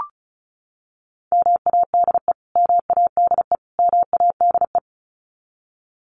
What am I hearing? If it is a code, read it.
Morse code, heard as a keyed tone, about 34 wpm, 700 Hz. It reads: MADE MADE MADE
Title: Top 500 Words Repeated 35wpm